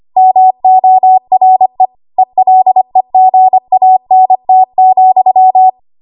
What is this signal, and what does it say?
Morse code, heard as a keyed tone, about 25 wpm, 750 Hz. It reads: MORE ELEGANT,
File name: Archive / 25 WPM / 25WPM